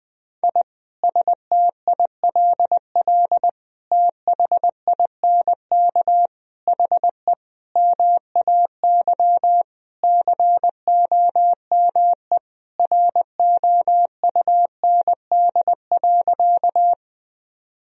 Morse code, heard as a keyed tone, about 20 wpm, 700 Hz. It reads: I STILL THINK HE MAY COME ROUND.
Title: Sentences from Top 1000 Words - Review Part 1 - 20wpm